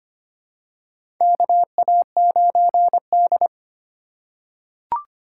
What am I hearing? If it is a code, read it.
Morse code, heard as a keyed tone, about 25 wpm, 700 Hz. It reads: KA9D
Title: USA Call Signs - 4 Characters - 8x Character Spacing - 25wpm